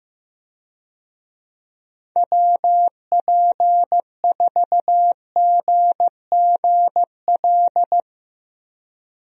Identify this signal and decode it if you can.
Morse code, heard as a keyed tone, about 16 wpm, 700 Hz. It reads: WP4GGL
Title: USA Call Signs - 6 Characters - Repeated 3x - 15wpm